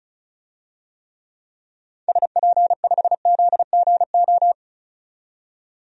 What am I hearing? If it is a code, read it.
Morse code, heard as a keyed tone, about 34 wpm, 700 Hz. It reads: SP5ZGO
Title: DXCC-100 Call Signs - 6 Characters - 35wpm